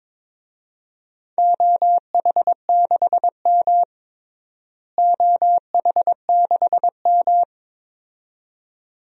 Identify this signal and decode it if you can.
Morse code, heard as a keyed tone, about 22 wpm, 700 Hz. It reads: OH6M OH6M
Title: DXCC-100 Call Signs - 4 Characters - Repeated 2x - 22wpm